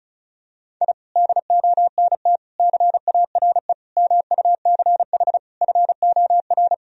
Morse code, heard as a keyed tone, about 36 wpm, 700 Hz. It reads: I DONT CARE MUCH FOR
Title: Sentences from Top 1000 Words - Review Part 4 - 35wpm